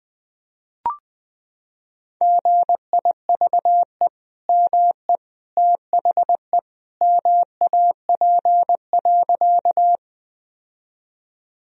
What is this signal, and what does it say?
Morse code, heard as a keyed tone, about 21 wpm, 700 Hz. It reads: GIVE ME THE MAP.
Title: Sentences from Top 1000 Words - Review Part 2 - 20wpm